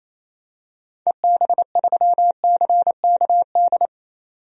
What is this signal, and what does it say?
Morse code, heard as a keyed tone, about 28 wpm, 700 Hz. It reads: EB3CKD
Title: DXCC-100 Call Signs - 6 Characters - 28wpm